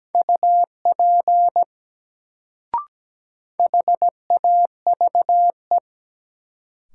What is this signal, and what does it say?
Morse code, heard as a keyed tone, about 17 wpm, 700 Hz. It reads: UP HAVE
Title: Sentences from Top 100 Words - New Word Drill - 17wpm